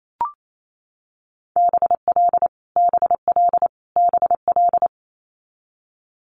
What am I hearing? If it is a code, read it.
Morse code, heard as a keyed tone, about 28 wpm, 700 Hz. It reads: BL BL BL
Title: Multi-letter Phonemes Repeated 3x 28wpm